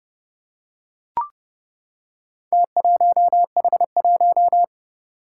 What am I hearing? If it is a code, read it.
Morse code, heard as a keyed tone, about 31 wpm, 700 Hz. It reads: T1H1
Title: Four Challenging Letter-Number Combinations 30wpm